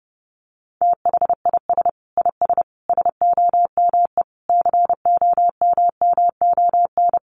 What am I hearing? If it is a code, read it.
Morse code, heard as a keyed tone, about 28 wpm, 700 Hz. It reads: THIS IS SOME COMMON
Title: Sentences from Top 1400 Words 30wpm